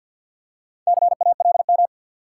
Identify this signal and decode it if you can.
Morse code, heard as a keyed tone, about 51 wpm, 700 Hz. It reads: CALM
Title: Four Letter Words - 50wpm